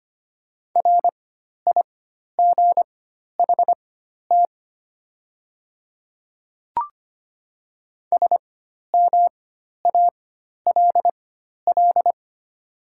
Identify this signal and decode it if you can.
Morse code, heard as a keyed tone, about 25 wpm, spaced as for 12 wpm, 700 Hz. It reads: RIGHT SMALL
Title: 4x Character Spacing - Top 200 Words - 25wpm